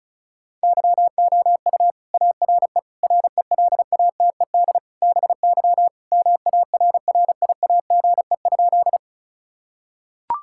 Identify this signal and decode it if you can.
Morse code, heard as a keyed tone, about 36 wpm, 700 Hz. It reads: YOU ARE RELATED BY MARRIAGE?
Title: Sentences from Top 1000 Words - Review Part 5 - 35wpm